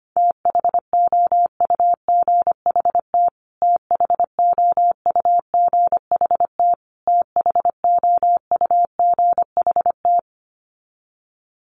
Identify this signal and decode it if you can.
Morse code, heard as a keyed tone, about 25 wpm, 700 Hz. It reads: THOUGHT THOUGHT THOUGHT
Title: Top 300 Words Repeated 25wpm